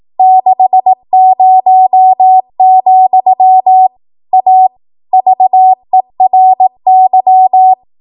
Morse code, heard as a keyed tone, about 18 wpm, 750 Hz. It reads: 60, A VERY
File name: Archive / 18 WPM / 18WPM